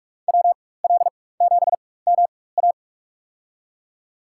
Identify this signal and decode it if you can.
Morse code, heard as a keyed tone, about 46 wpm, spaced as for 21 wpm, 700 Hz. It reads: WL7MA
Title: USA Call Signs - 5 Characters - 4x Character Spacing - 45wpm